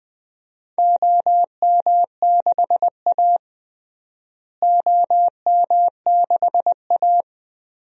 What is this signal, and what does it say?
Morse code, heard as a keyed tone, about 20 wpm, 700 Hz. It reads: OM6A OM6A
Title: DXCC-100 Call Signs - 4 Characters - Repeated 3x - 20wpm